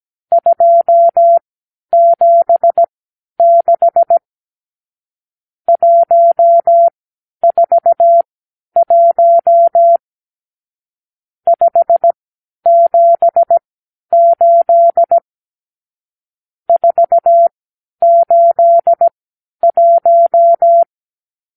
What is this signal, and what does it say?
Morse code, heard as a keyed tone, about 17 wpm, spaced as for 10 wpm, 680 Hz. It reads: 276 141 578 481
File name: k7qo / Lesson 54 (9)